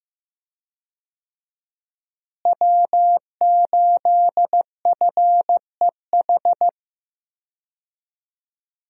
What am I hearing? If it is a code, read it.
Morse code, heard as a keyed tone, about 16 wpm, 700 Hz. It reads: W8FEH
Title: USA Call Signs - 5 Characters - Repeated 2x - 15wpm